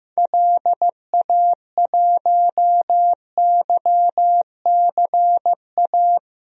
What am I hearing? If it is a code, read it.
Morse code, heard as a keyed tone, about 15 wpm, 700 Hz. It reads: LA1YCA
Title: DXCC-100 Call Signs - 6 Characters - 15wpm